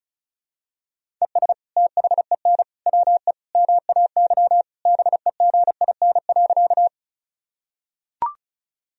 Morse code, heard as a keyed tone, about 36 wpm, 700 Hz. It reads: ES THEN WE MAY BEGIN.
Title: Sentences from Top 300 Words and Abbreviations 35wpm